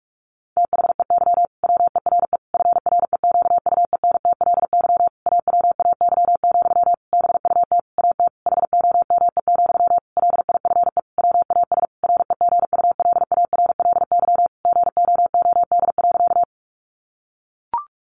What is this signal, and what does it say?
Morse code, heard as a keyed tone, about 44 wpm, 700 Hz. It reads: THEY WERE FREQUENTLY AWAY, BUT AT HOME, LIFE WAS REGULARLY GOOD.
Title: Sentences from Top 1400 Words 45wpm